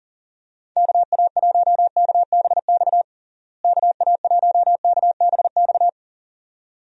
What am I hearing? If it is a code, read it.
Morse code, heard as a keyed tone, about 40 wpm, 700 Hz. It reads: KA1KBX KA1KBX
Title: USA Call Signs - 6 Characters - Repeated 2x - 40wpm